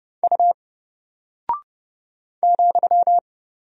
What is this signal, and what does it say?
Morse code, heard as a keyed tone, about 30 wpm, 700 Hz. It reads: U ,
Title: Single Character (Letter:Number:Punctuation:Prosign) 30wpm